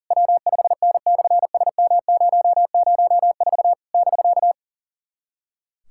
Morse code, heard as a keyed tone, about 40 wpm, 700 Hz. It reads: W5N/SM004 <BK>
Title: SOTA QSO Practice 40wpm